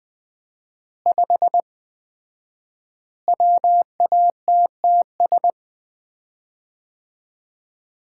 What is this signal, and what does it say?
Morse code, heard as a keyed tone, about 20 wpm, 700 Hz. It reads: 5 WATTS
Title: Common QSO Phrases - 4x Word-Spacing - 20wpm